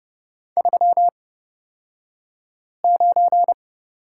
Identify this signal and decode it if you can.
Morse code, heard as a keyed tone, about 30 wpm, spaced as for 5 wpm, 700 Hz. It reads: 39
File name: Lesson 36d - BT - Two Characters 30wpm